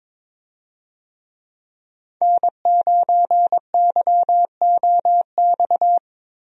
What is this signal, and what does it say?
Morse code, heard as a keyed tone, about 22 wpm, 700 Hz. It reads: N9YOX
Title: USA Call Signs - 5 Characters - 22wpm